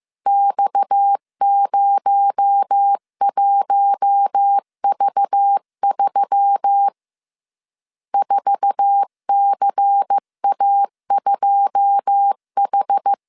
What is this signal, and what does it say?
Morse code, heard as a keyed tone, about 15 wpm, 800 Hz. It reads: X01V3 4CA2H
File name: CW / DET / 1 9 1500C S C 14wpm 3245